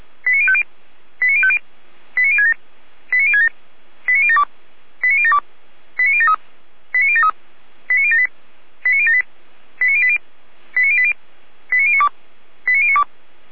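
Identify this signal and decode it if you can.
Five-tone selective call (ZVEI), heard as quick runs of five tones, tones of about 70 ms. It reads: 89050 89050 89067 89067 89072 89072 89073 89073 89078 89078 89080 89080 89092 89092
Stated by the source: ZVEI five-tone selective call received on an SDR.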